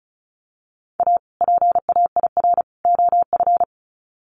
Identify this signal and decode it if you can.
Morse code, heard as a keyed tone, about 34 wpm, 700 Hz. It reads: A PAIR OF
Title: Sets of 3 Words 35wpm